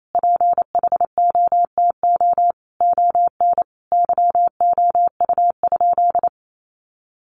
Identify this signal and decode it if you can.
Morse code, heard as a keyed tone, about 28 wpm, 700 Hz. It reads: PHOTO ON YOU?
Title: Sentences from Top 1200 Words 28wpm